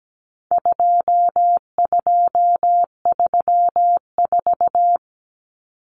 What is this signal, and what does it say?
Morse code, heard as a keyed tone, about 17 wpm, 700 Hz. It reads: 2234
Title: Common Cut Numbers - 4-Digit 17wpm